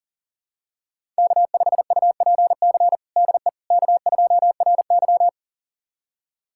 Morse code, heard as a keyed tone, about 37 wpm, 700 Hz. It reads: K5UPC DE K2RY
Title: USA Callsign DE Callsign 40wpm